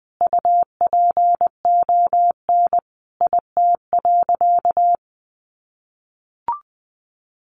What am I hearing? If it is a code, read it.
Morse code, heard as a keyed tone, about 20 wpm, 700 Hz. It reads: UPON IT.
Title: Sentences from Top 1300 Words 20wpm